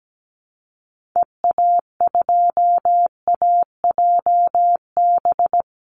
Morse code, heard as a keyed tone, about 17 wpm, 700 Hz. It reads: EA2AJB
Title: Call Signs Repeated 2x - 17wpm